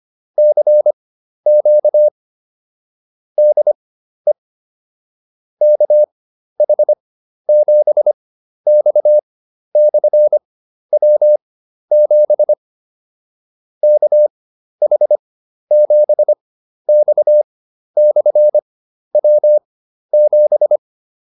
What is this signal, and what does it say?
Morse code, heard as a keyed tone, about 25 wpm, spaced as for 12 wpm, 600 Hz. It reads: CQ DE KH7X/W7 KH7X/W7
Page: fundamental-practice-files